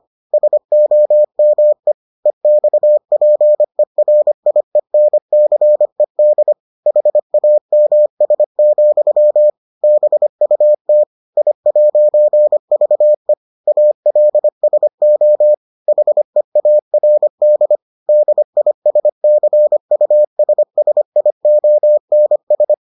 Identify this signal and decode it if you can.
Morse code, heard as a keyed tone, about 25 wpm, 600 Hz. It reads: SOME EXPERIENCED HAMS, BUT I'VE ALSO HEARD DISCUSSIONS